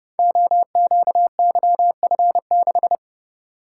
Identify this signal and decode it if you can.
Morse code, heard as a keyed tone, about 30 wpm, 700 Hz. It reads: OQYF6
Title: Code Groups - 5 Letter:Numbers - 30wpm